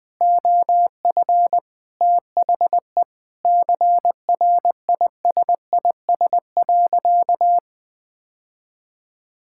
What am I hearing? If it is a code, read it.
Morse code, heard as a keyed tone, about 20 wpm, 700 Hz. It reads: OF THE CRISIS.
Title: Sentences with 8-Word Phrases - 20wpm